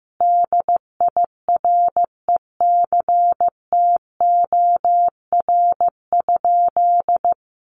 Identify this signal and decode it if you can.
Morse code, heard as a keyed tone, about 15 wpm, 700 Hz. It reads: DIRECTOR?
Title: Sentences from Top 1300 Words 15wpm